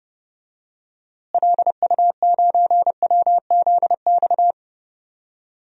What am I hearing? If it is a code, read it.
Morse code, heard as a keyed tone, about 30 wpm, 700 Hz. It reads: LU9WZX
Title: DXCC-100 Call Signs - 6 Characters - Repeated 3x - 30wpm